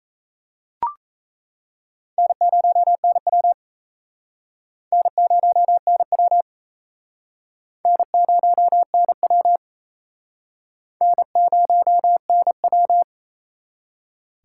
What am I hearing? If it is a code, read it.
Morse code, heard as a keyed tone, about 36 wpm, 700 Hz. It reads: N0NW N0NW N0NW N0NW